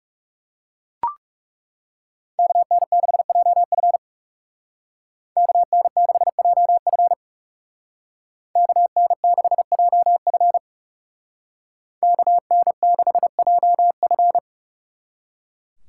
Morse code, heard as a keyed tone, about 39 wpm, 700 Hz. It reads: KN6JF KN6JF KN6JF KN6JF